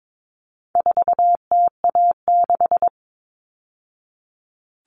Morse code, heard as a keyed tone, about 22 wpm, 700 Hz. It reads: 4TA6
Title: All Cut Numbers - 4-Digit 22wpm